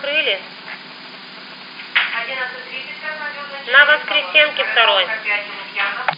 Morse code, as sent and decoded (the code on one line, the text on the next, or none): none